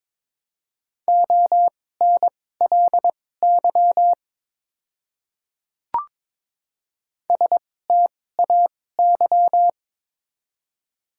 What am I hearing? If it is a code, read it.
Morse code, heard as a keyed tone, about 21 wpm, spaced as for 16 wpm, 700 Hz. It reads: ONLY STAY
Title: 2x Character Spacing - Top 500 Words - 22wpm